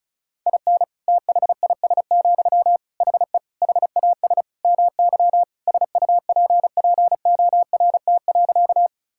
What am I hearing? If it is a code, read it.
Morse code, heard as a keyed tone, about 36 wpm, 700 Hz. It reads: IN THIS, HE HAS MY SUPPORT.
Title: Sentences from Top 1000 Words - Review Part 5 - 35wpm